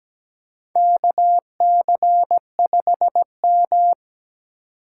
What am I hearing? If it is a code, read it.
Morse code, heard as a keyed tone, about 17 wpm, 700 Hz. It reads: KC5M
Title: USA Call Signs - 4x Character Spacing - 17wpm